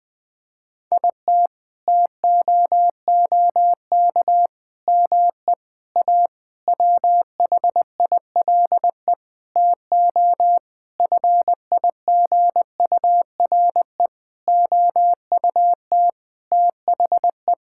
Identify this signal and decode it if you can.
Morse code, heard as a keyed tone, about 21 wpm, 700 Hz. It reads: IT TOOK ME A WHILE TO FIGURE OUT THE
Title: Sentences with 8-Word Phrases - 20wpm